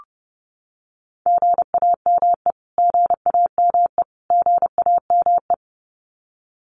Morse code, heard as a keyed tone, about 30 wpm, 700 Hz. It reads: GAME GAME GAME
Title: Top 500 Words Repeated 30wpm